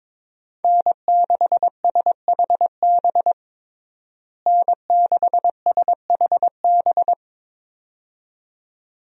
Morse code, heard as a keyed tone, about 22 wpm, 700 Hz. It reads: N6SHB N6SHB
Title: USA Call Signs - 5 Characters - Repeated 2x - 22wpm